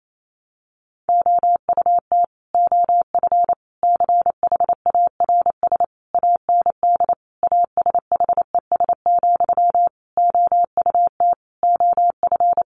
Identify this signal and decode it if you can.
Morse code, heard as a keyed tone, about 28 wpm, 700 Hz. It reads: OUT OF CHARS AND ASHES, OUT OF
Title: A Sound Of Thunder 28wpm